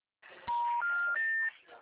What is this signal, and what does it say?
Special information tone, with speech.